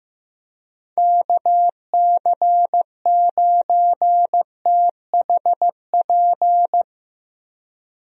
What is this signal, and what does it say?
Morse code, heard as a keyed tone, about 15 wpm, 700 Hz. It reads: KC9THP
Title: USA Call Signs - 6 Characters - Repeated 2x - 15wpm